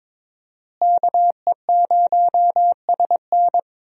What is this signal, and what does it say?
Morse code, heard as a keyed tone, about 22 wpm, 700 Hz. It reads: KE0SN
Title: USA Call Signs - 5 Characters - Repeated 3x - 22wpm